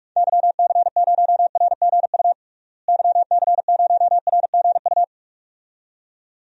Morse code, heard as a keyed tone, about 46 wpm, 700 Hz. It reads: YC0RGU YC0RGU
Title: DXCC-100 Call Signs - 6 Characters - Repeated 2x - 45wpm